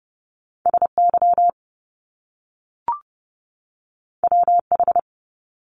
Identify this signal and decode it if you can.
Morse code, heard as a keyed tone, about 30 wpm, 700 Hz. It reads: SY WH
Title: Top Two Letter Combinations in Words 30wpm